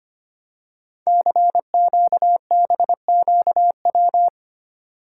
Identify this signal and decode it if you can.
Morse code, heard as a keyed tone, about 25 wpm, 690 Hz. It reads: CQBQW
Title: Code Groups - 5 Letter:Numbers - 25wpm